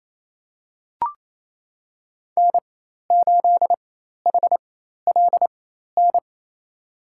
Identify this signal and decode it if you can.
Morse code, heard as a keyed tone, about 28 wpm, spaced as for 13 wpm, 700 Hz. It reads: N8HLN
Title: USA Call Signs - 4x Character Spacing - 28wpm